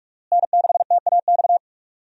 Morse code, heard as a keyed tone, about 46 wpm, 700 Hz. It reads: N6TAX
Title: USA Call Signs - 5 Characters - Repeated 3x - 45wpm